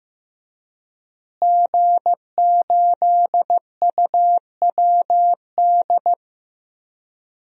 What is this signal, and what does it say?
Morse code, heard as a keyed tone, about 15 wpm, 700 Hz. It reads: G8UWD
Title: DXCC-100 Call Signs - 5 Characters - Repeated 2x - 15wpm